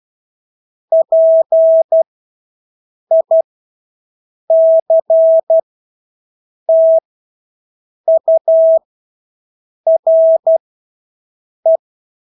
Morse code, heard as a keyed tone, about 12 wpm, spaced as for 6 wpm, 650 Hz. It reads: PICTURE
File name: Engelske ord 12-6 wpm